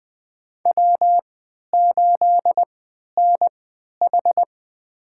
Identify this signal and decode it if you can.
Morse code, heard as a keyed tone, about 20 wpm, spaced as for 11 wpm, 700 Hz. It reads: W8NH
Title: USA Call Signs - 3x Character Spacing - 20wpm